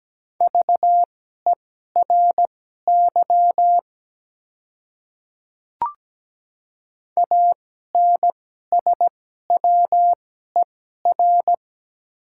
Morse code, heard as a keyed tone, about 17 wpm, spaced as for 12 wpm, 700 Hz. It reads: VERY ANSWER
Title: 2x Character Spacing - Top 300 Words - 17wpm